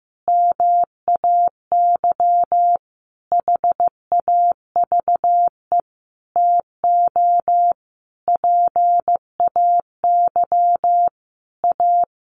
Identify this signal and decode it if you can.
Morse code, heard as a keyed tone, about 15 wpm, 700 Hz. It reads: MAY HAVE TO PAY A
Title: Sentences from Top 1100 Words 15wpm